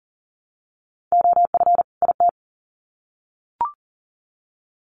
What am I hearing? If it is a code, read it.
Morse code, heard as a keyed tone, about 40 wpm, 700 Hz. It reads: OF IT
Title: Sets of 2 Words 40wpm